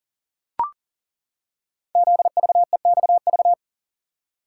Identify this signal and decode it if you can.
Morse code, heard as a keyed tone, about 42 wpm, 690 Hz. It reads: ZVEXV
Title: Code Groups - 5 Letters - 40wpm